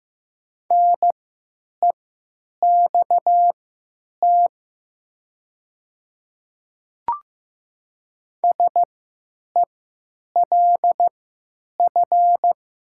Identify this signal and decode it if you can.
Morse code, heard as a keyed tone, about 15 wpm, spaced as for 9 wpm, 700 Hz. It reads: NEXT SELF
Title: 3x Character Spacing - Top 300 Words - 15wpm